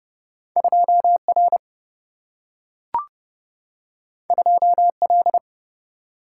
Morse code, heard as a keyed tone, about 30 wpm, 700 Hz. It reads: 2 R 2 L
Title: Single-Letter Single-Number 30wpm